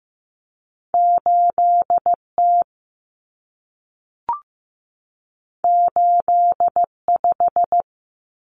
Morse code, heard as a keyed tone, about 15 wpm, 700 Hz. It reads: 8T 85
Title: Common Cut Numbers - 2-Digit 15wpm